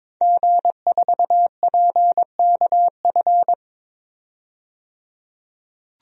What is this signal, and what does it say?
Morse code, heard as a keyed tone, about 22 wpm, 700 Hz. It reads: G4PKF
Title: DXCC-100 Call Signs - 5 Characters - 22wpm